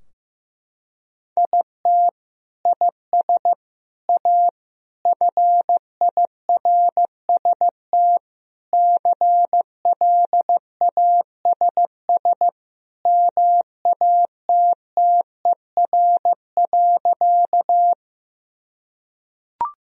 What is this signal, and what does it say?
Morse code, heard as a keyed tone, about 15 wpm, 700 Hz. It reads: IT IS A FIRST CLASS MATTER.
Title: Sentences from Top 1000 Words - Review Part 2 - 15wpm